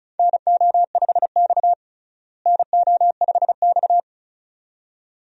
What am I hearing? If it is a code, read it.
Morse code, heard as a keyed tone, about 34 wpm, 700 Hz. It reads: NO5X NO5X
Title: USA Call Signs - 4 Characters - Repeated 2x - 35wpm